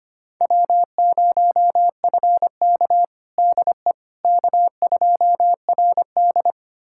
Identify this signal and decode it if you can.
Morse code, heard as a keyed tone, about 25 wpm, 700 Hz. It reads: W0FK DE K2RD
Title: USA Callsign DE Callsign - 4 Characters - 25wpm